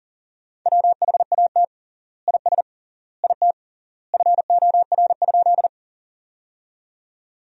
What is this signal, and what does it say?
Morse code, heard as a keyed tone, about 37 wpm, 700 Hz. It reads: WHAT IS IT FOR?
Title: 3x Word-Spacing - Sentences from Top 100 Words - 40wpm